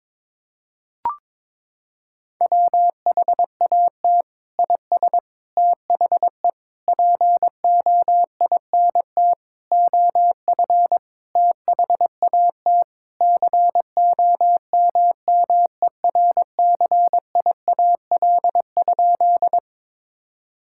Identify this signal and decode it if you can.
Morse code, heard as a keyed tone, about 22 wpm, 700 Hz. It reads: WHAT IS THE POINT OF THAT COMMERCIAL?
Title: Sentences from Top 1000 Words - Review Part 5 - 22wpm